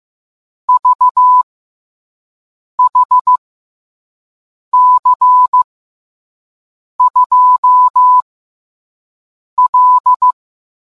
Morse code, heard as a keyed tone, about 15 wpm, 1000 Hz. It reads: V H C 2 L